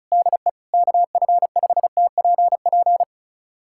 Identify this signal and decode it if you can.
Morse code, heard as a keyed tone, about 34 wpm, 700 Hz. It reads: DE KF5TPP
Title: USA Callsign DE Callsign - 6 Characters - 35wpm